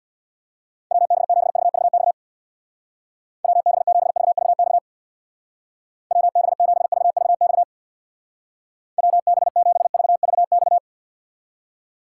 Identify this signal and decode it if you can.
Morse code, heard as a keyed tone, about 64 wpm, 700 Hz. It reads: WB7VVX WB7VVX WB7VVX WB7VVX